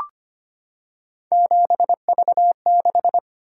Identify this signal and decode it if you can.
Morse code, heard as a keyed tone, about 25 wpm, 700 Hz. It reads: 7V6
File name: Three Letter-Number Combinations - 25wpm